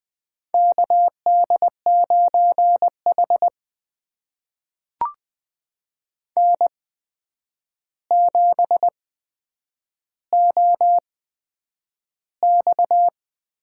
Morse code, heard as a keyed tone, about 20 wpm, spaced as for 9 wpm, 700 Hz. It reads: KD9H N7OX
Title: USA Call Signs - 4 Characters - 8x Character Spacing - 20wpm